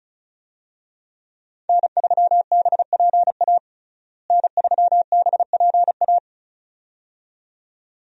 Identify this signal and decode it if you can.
Morse code, heard as a keyed tone, about 36 wpm, 700 Hz. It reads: N3BPA N3BPA
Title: USA Call Signs - 5 Characters - Repeated 2x - 35wpm